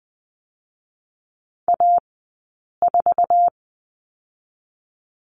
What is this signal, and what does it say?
Morse code, heard as a keyed tone, about 20 wpm, 700 Hz. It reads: A 4
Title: 113 Lesson 19d - Letter w - Two Characters 20wpm